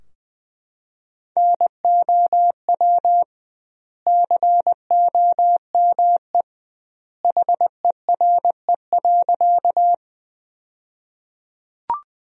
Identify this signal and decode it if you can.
Morse code, heard as a keyed tone, about 20 wpm, 700 Hz. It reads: NOW COME HERE.
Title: 2x Word-Spacing - Sentences from Top 200 Words - 20wpm